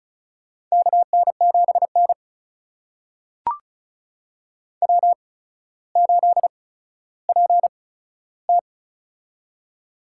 Morse code, heard as a keyed tone, about 36 wpm, spaced as for 15 wpm, 700 Hz. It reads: KN7N W8PT
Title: USA Call Signs - 4 Characters - 8x Character Spacing - 35wpm